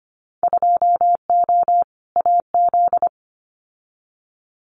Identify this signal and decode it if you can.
Morse code, heard as a keyed tone, about 25 wpm, 700 Hz. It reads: 2O AZ
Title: Winter Field Day Exchanges 25wpm